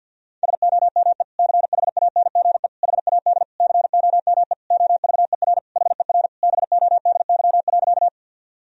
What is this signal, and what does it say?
Morse code, heard as a keyed tone, about 46 wpm, 700 Hz. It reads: SOME CHANGE HAD COME OVER HER BODY.
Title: Sentences from Top 1000 Words - Review Part 2 - 50wpm